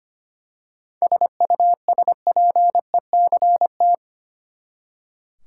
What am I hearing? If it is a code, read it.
Morse code, heard as a keyed tone, about 25 wpm, 700 Hz. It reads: SUSPECT
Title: Seven Letter Words - 25wpm